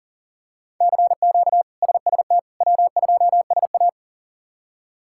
Morse code, heard as a keyed tone, about 40 wpm, 700 Hz. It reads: CQ SST W2SA
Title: K1USN SST QSO Practice - 40wpm